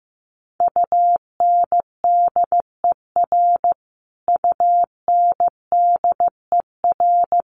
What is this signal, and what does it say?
Morse code, heard as a keyed tone, about 15 wpm, 700 Hz. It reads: UNDER UNDER
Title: Top 500 Words Repeated 15wpm